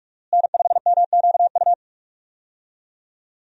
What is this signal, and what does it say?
Morse code, heard as a keyed tone, about 44 wpm, 700 Hz. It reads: N5MQU
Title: USA Call Signs - 5 Characters - Repeated 2x - 45wpm